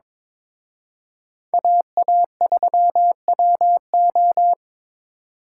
Morse code, heard as a keyed tone, about 27 wpm, 700 Hz. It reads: AA3WO
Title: Speed Racing - USA 2x2 Call Signs 22wpm